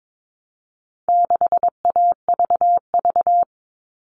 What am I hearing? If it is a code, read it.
Morse code, heard as a keyed tone, about 22 wpm, 700 Hz. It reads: 6AVV
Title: All Cut Numbers - 4-Digit 22wpm